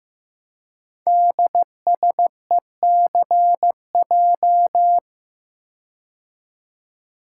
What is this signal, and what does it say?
Morse code, heard as a keyed tone, about 15 wpm, 700 Hz. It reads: DSECJ
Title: Code Groups - 5 Letters - 15wpm